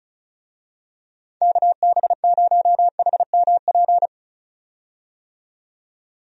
Morse code, heard as a keyed tone, about 36 wpm, 700 Hz. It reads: KB0HMP
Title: USA Call Signs - 6 Characters - 35wpm